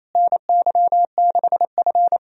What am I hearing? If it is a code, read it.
Morse code, heard as a keyed tone, about 28 wpm, 700 Hz. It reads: NY6F
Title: USA Call Signs - 4 Characters - Repeated 3x - 28wpm